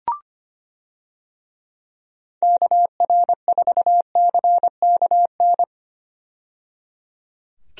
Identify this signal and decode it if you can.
Morse code, heard as a keyed tone, about 25 wpm, 700 Hz. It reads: KR4CKN
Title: USA Call Signs - 6 Characters - 25wpm